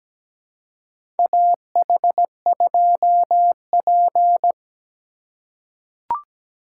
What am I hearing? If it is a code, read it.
Morse code, heard as a keyed tone, about 17 wpm, 700 Hz. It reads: AH2P
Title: USA Call Signs - 4 Characters - 2x Character Spacing - 17wpm